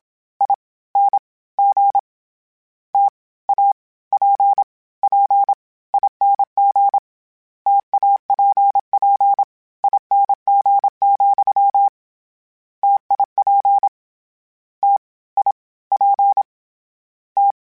Morse code, heard as a keyed tone, about 27 wpm, 800 Hz. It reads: I N G T A P P ING TAPP ING, TIP T I P T
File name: Variable Spacing / Variable Spacing-mp3s / T AOEs @25